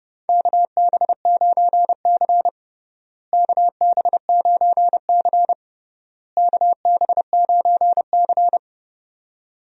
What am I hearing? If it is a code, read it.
Morse code, heard as a keyed tone, about 30 wpm, 700 Hz. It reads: KB9C KB9C KB9C
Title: USA Call Signs - 4 Characters - Repeated 3x - 30wpm